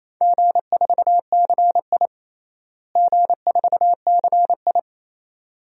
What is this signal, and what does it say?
Morse code, heard as a keyed tone, about 28 wpm, 700 Hz. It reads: G4CI G4CI
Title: DXCC-100 Call Signs - 4 Characters - Repeated 2x - 28wpm